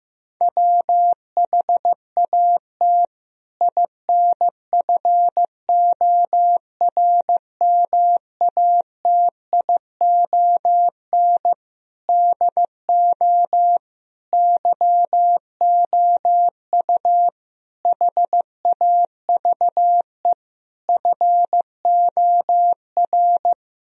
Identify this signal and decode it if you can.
Morse code, heard as a keyed tone, about 15 wpm, 700 Hz. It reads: WHAT INFORMATION DO YOU HAVE FOR
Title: Sentences from Top 1000 Words - Review Part 3 - 15wpm